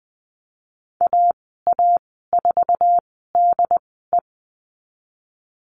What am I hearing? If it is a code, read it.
Morse code, heard as a keyed tone, about 20 wpm, spaced as for 14 wpm, 700 Hz. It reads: AA4DE
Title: 076 Lesson 13g - Letter L - Call Signs 20wpm